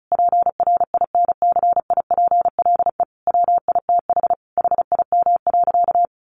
Morse code, heard as a keyed tone, about 34 wpm, 700 Hz. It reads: PRINCIPLE WITH HIM.
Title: Sentences from Top 1400 Words 35wpm